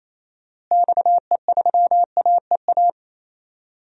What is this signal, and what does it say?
Morse code, heard as a keyed tone, about 28 wpm, 700 Hz. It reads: XE3AEA
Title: DXCC-100 Call Signs - 6 Characters - Repeated 3x - 28wpm